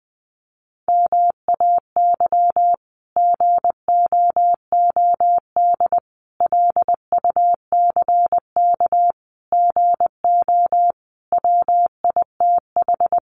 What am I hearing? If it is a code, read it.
Morse code, heard as a keyed tone, about 20 wpm, 700 Hz. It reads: MAY GOOD LUCK GO WITH
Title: Sentences from Top 1500 Words 20wpm